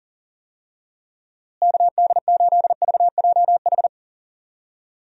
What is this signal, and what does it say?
Morse code, heard as a keyed tone, about 40 wpm, 700 Hz. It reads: KD8VJH
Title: USA Call Signs - 6 Characters - Repeated 2x - 40wpm